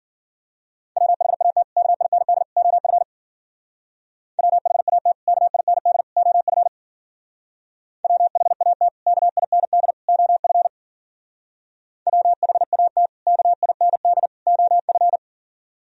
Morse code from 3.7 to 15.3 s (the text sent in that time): WHAT KIND OF WHAT KIND OF WHAT KIND OF